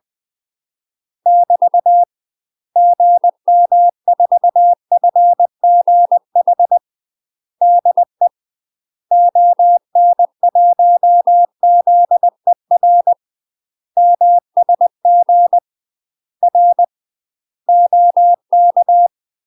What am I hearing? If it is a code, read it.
Morse code, heard as a keyed tone, about 20 wpm, 700 Hz. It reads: <BT> GM4FGH DE ON1ZER MSG R OK
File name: UK MASTER QSOs TRAINING / 20WPM / UK MASTER QSOs 0065